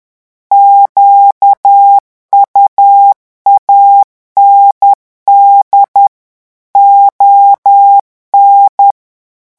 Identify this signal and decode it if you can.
Morse code, heard as a keyed tone, about 10 wpm, 800 Hz. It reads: QUAND ON
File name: morse mp3 / revision francais 10 mots minute